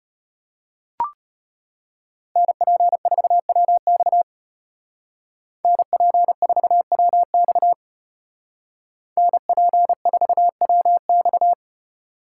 Morse code, heard as a keyed tone, about 33 wpm, 700 Hz. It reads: NP4WX NP4WX NP4WX